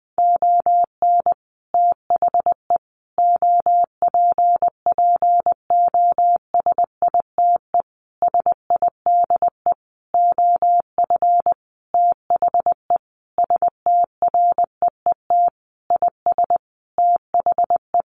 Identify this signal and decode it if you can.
Morse code, heard as a keyed tone, about 20 wpm, 700 Hz. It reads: ON THE OPPOSITE SIDE OF THE STREET IS THE
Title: Sentences from Top 1400 Words 20wpm